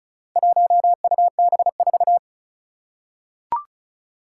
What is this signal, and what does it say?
Morse code, heard as a keyed tone, about 36 wpm, 700 Hz. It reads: 1UB4
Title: Four Challenging Letter-Number Combinations 35wpm